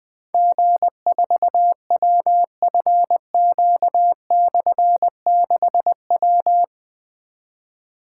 Morse code, heard as a keyed tone, about 20 wpm, 700 Hz. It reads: G4WFQ/6W
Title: Difficult Call Signs 20wpm